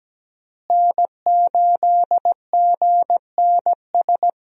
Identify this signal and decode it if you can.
Morse code, heard as a keyed tone, about 17 wpm, 700 Hz. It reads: N8GNS
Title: USA Call Signs - 5 Characters - Repeated 2x - 17wpm